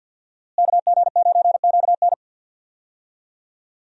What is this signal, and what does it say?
Morse code, heard as a keyed tone, about 49 wpm, 700 Hz. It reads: KG9QN
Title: USA Call Signs - 5 Characters - Repeated 2x - 50wpm